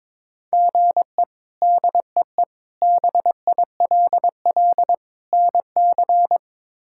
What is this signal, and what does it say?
Morse code, heard as a keyed tone, about 21 wpm, 700 Hz. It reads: GE DEE BILL NC
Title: K1USN SST QSO Practice - 22wpm